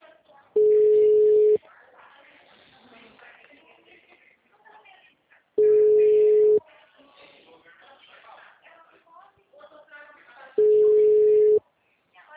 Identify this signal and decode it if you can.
Ringback tone (Europe, 425 Hz), with speech.